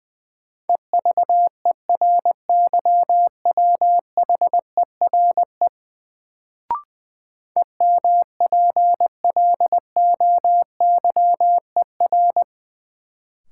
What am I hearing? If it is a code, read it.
Morse code, heard as a keyed tone, about 20 wpm, 700 Hz. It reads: EVERYWHERE EMPLOYER
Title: Sentences from Top 1600 Words - New Word Drill - 20wpm